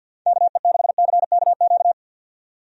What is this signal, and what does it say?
Morse code, heard as a keyed tone, about 51 wpm, 700 Hz. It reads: KE6CKQ
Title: USA Call Signs - 6 Characters - 50wpm